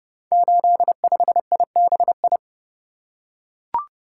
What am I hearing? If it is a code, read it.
Morse code, heard as a keyed tone, about 30 wpm, 700 Hz. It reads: 85IBI
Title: Code Groups - 5 Letter:Numbers - 30wpm